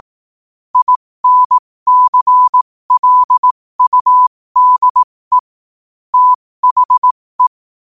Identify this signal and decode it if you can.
Morse code, heard as a keyed tone, about 18 wpm, spaced as for 15 wpm, 1000 Hz. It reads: INCLUDE THE
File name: qotdfiles / 15wpm